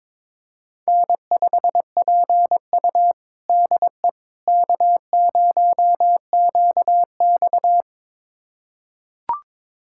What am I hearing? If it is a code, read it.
Morse code, heard as a keyed tone, about 22 wpm, 700 Hz. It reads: N5PU DE K0QX
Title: USA Callsign DE Callsign - 4 Characters - 22wpm